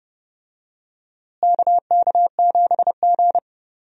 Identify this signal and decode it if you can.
Morse code, heard as a keyed tone, about 30 wpm, 700 Hz. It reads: KK7G
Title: USA Call Signs - 4 Characters - Repeated 3x - 30wpm